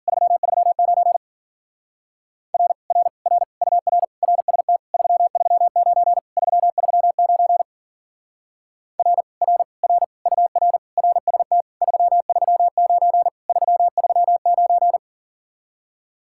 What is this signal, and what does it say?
Morse code, heard as a keyed tone, about 48 wpm, 700 Hz. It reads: 339 R R R UR RST 339 339 R R R UR RST 339 339
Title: Speed-Racing Common QSO Phrases - 40wpm